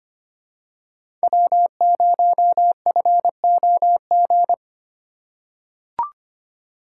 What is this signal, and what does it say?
Morse code, heard as a keyed tone, about 25 wpm, 700 Hz. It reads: W0FOG
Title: USA Call Signs - 5 Characters - 4x Character Spacing - 25wpm